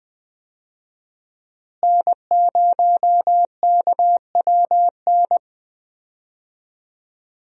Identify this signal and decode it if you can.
Morse code, heard as a keyed tone, about 20 wpm, 700 Hz. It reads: N0KWN
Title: USA Call Signs - 5 Characters - Repeated 2x - 20wpm